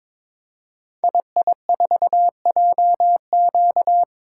Morse code, heard as a keyed tone, about 22 wpm, 700 Hz. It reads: II4JQ
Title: Code Groups - 5 Letter:Numbers - 22wpm